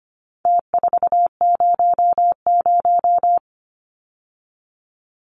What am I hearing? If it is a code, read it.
Morse code, heard as a keyed tone, about 25 wpm, 700 Hz. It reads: T400
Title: Numbers - Contest Style - 4-Digit 25wpm